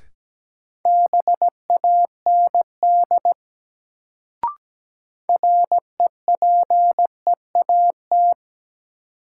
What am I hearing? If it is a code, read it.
Morse code, heard as a keyed tone, about 17 wpm, 700 Hz. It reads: BAND REPEAT
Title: Sentences from Top 1600 Words - New Word Drill - 17wpm